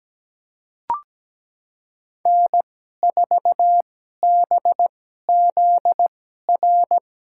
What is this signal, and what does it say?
Morse code, heard as a keyed tone, about 17 wpm, spaced as for 12 wpm, 700 Hz. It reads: N4BZR
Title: USA Call Signs - 5 Characters - 2x Character Spacing - 17wpm